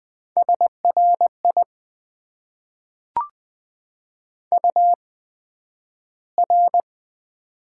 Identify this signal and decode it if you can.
Morse code, heard as a keyed tone, about 20 wpm, spaced as for 11 wpm, 700 Hz. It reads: SRI UR
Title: Top 90 QSO Elements - 8x Character Spacing - 20wpm